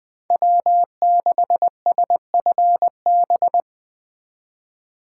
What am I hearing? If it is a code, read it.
Morse code, heard as a keyed tone, about 20 wpm, 700 Hz. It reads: W6SFB